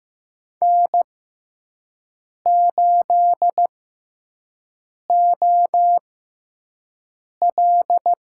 Morse code, heard as a keyed tone, about 15 wpm, spaced as for 5 wpm, 700 Hz. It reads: N8OL
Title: USA Call Signs - 5 Characters - 6x Character Spacing - 15wpm